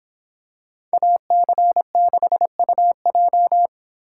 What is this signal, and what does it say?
Morse code, heard as a keyed tone, about 26 wpm, 700 Hz. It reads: AC6UJ